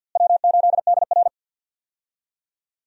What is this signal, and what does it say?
Morse code, heard as a keyed tone, about 52 wpm, 700 Hz. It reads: W8DR
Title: USA Call Signs - 4 Characters - Repeated 2x - 50wpm